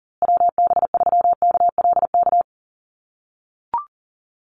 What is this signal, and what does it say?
Morse code, heard as a keyed tone, about 40 wpm, 700 Hz. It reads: WB3KLK
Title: USA Call Signs Repeated 3x 40wpm